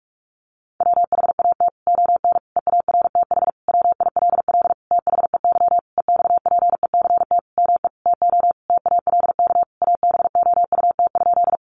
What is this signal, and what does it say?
Morse code, heard as a keyed tone, about 44 wpm, 700 Hz. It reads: WHAT ON EARTH WILL THEY EXPECT ME TO TALK ABOUT?
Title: Sentences from Top 1100 Words 45wpm